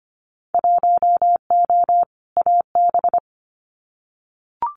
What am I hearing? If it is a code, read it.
Morse code, heard as a keyed tone, about 25 wpm, 690 Hz. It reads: 1O AB
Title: Winter Field Day Exchanges 25wpm